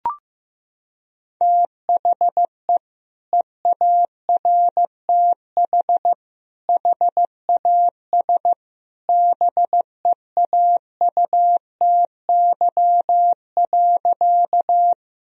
Morse code, heard as a keyed tone, about 15 wpm, 700 Hz. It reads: THE EARTH HAS BEAUTY.
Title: Sentences from Top 1000 Words - Review Part 2 - 15wpm